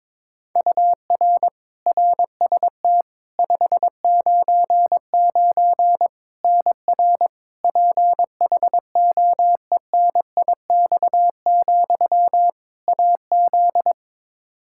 Morse code, heard as a keyed tone, about 21 wpm, 700 Hz. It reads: UR RST 599 NR PHOENIX, AZ
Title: Practice QSO - Exchange 1 - Beginning - 22wpm